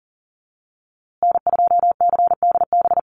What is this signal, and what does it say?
Morse code, heard as a keyed tone, about 40 wpm, 700 Hz. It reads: N2CDB
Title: USA Call Signs Repeated 2x 40wpm